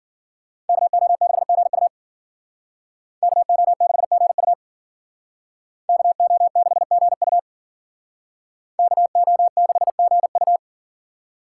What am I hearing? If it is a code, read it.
Morse code, heard as a keyed tone, about 52 wpm, 700 Hz. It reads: KO6GU KO6GU KO6GU KO6GU